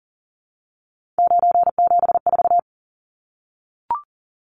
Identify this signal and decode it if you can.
Morse code, heard as a keyed tone, about 40 wpm, 700 Hz. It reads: 974
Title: Numbers - 3-Digit 40wpm